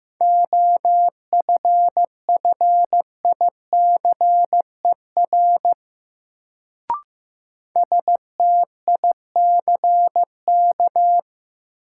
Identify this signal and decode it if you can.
Morse code, heard as a keyed tone, about 15 wpm, 700 Hz. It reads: OFFICER STICK
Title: Sentences from Top 1200 Words - New Word Drill Alt - 15wpm